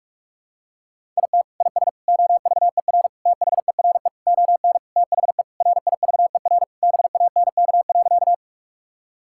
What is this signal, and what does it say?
Morse code, heard as a keyed tone, about 41 wpm, 700 Hz. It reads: IT IS OVER THERE ON THE RIVER BANK.
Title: Sentences from Top 1000 Words - Review Part 6 - 45wpm